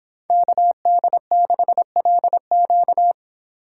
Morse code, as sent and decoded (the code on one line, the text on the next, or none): -.- -.. -.... .-.. --.-
KD6LQ